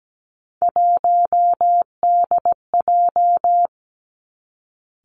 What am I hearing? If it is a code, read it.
Morse code, heard as a keyed tone, about 17 wpm, 700 Hz. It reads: 1DJ
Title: Three Challenging Letter-Number Combinations 17wpm